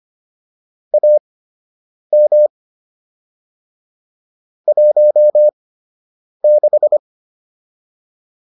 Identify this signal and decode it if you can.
Morse code, heard as a keyed tone, about 25 wpm, spaced as for 8 wpm, 600 Hz. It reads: AM 16